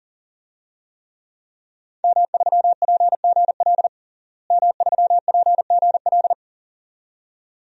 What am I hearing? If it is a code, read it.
Morse code, heard as a keyed tone, about 40 wpm, 700 Hz. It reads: M3PGL M3PGL
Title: DXCC-100 Call Signs - 5 Characters - Repeated 2x - 40wpm